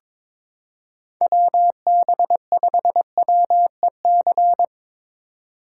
Morse code, heard as a keyed tone, about 22 wpm, 700 Hz. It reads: WB5WEC
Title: K1USN SST QSO Practice - 22wpm